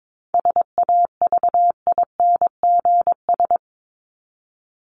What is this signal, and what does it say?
Morse code, heard as a keyed tone, about 22 wpm, 700 Hz. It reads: SAVINGS
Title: Top 1001-1250 Words 22wpm